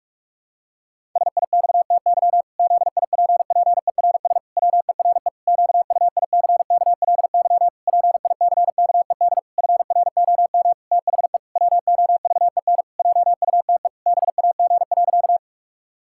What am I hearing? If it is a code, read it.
Morse code, heard as a keyed tone, about 46 wpm, 700 Hz. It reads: SIXTY ZIPPERS WERE QUICKLY PICKED FROM THE WOVEN JUTE BAG.
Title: Pangram Challenge v2 - 45wpm